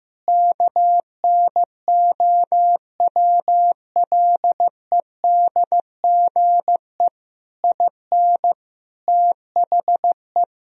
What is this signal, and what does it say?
Morse code, heard as a keyed tone, about 15 wpm, 700 Hz. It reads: KNOWLEDGE IN THE
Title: Sentences from Top 1000 Words - Review Part 4 - 15wpm